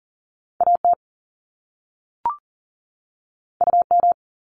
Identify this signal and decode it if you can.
Morse code, heard as a keyed tone, about 38 wpm, 700 Hz. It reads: AT UM
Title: Top Two Letter Combinations in Words 40wpm